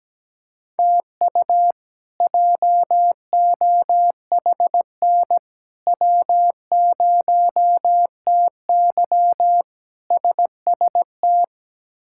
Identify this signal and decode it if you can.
Morse code, heard as a keyed tone, about 17 wpm, 700 Hz. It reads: TU JOHN W0TY SST
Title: K1USN SST QSO Practice - 17wpm